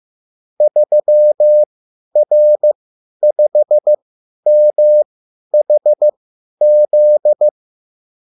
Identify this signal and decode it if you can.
Morse code, heard as a keyed tone, about 15 wpm, spaced as for 10 wpm, 600 Hz. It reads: 3R5MHZ
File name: QSL / CW lesson